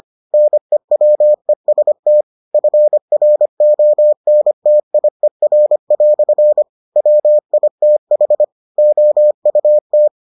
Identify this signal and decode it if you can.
Morse code, heard as a keyed tone, about 25 wpm, 600 Hz. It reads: NEWEST FRONTIER" WITH OUT